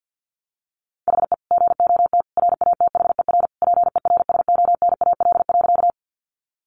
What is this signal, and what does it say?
Morse code, heard as a keyed tone, about 50 wpm, 700 Hz. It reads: HE GOT RATHER PERSONAL.
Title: Sentences from Top 700 Words 50wpm